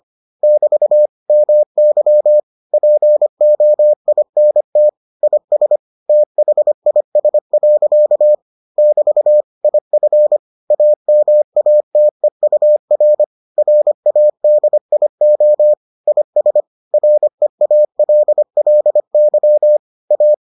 Morse code, heard as a keyed tone, about 25 wpm, 600 Hz. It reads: <BT> MY POINT IS THIS. <BT> IF AMATEUR RADIO IS REALLY A